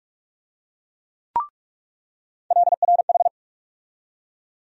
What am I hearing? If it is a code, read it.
Morse code, heard as a keyed tone, about 42 wpm, 700 Hz. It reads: LRH
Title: USA Call Sign Postfixes - 3 Letters 45wpm